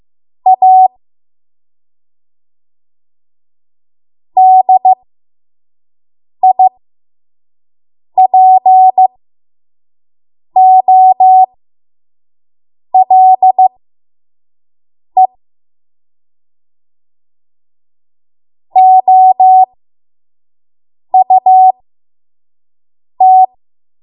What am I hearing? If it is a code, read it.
Morse code, heard as a keyed tone, about 15 wpm, spaced as for 5 wpm, 750 Hz. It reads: A DIPOLE OUT
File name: Morse / 05WPM